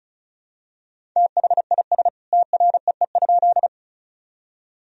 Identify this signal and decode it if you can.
Morse code, heard as a keyed tone, about 34 wpm, 700 Hz. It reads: THIS TREE?
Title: Sentences from Top 1000 Words - Review Part 1 - 35wpm